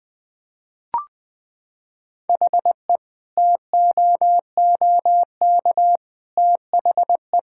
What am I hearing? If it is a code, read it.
Morse code, heard as a keyed tone, about 20 wpm, 700 Hz. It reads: HE TOOK THE
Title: Sentences with 3-Word Phrases - 20wpm